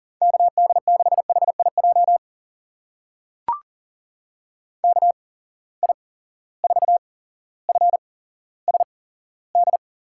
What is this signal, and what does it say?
Morse code, heard as a keyed tone, about 40 wpm, spaced as for 17 wpm, 700 Hz. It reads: KD6HIJ KI4FSD
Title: USA Call Signs - 8x Character Spacing - 40wpm